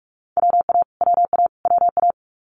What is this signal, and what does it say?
Morse code, heard as a keyed tone, about 46 wpm, 700 Hz. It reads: WA WA WA
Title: Multi-letter Phonemes Repeated 3x 45wpm